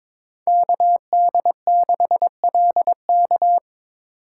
Morse code, heard as a keyed tone, about 23 wpm, 700 Hz. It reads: KD6LK
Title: USA Call Signs - 5 Characters - 3x Character Spacing - 22wpm